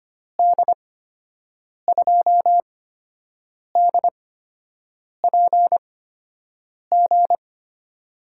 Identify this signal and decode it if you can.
Morse code, heard as a keyed tone, about 25 wpm, spaced as for 7 wpm, 700 Hz. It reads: D2DPG
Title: USA Call Signs - 8x Character Spacing - 25wpm